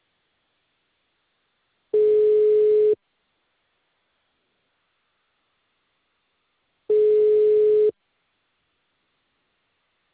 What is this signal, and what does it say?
Ringback tone (Europe, 425 Hz).